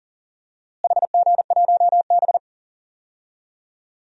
Morse code, heard as a keyed tone, about 39 wpm, 700 Hz. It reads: HG1B
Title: DXCC-100 Call Signs - 4 Characters - Repeated 3x - 40wpm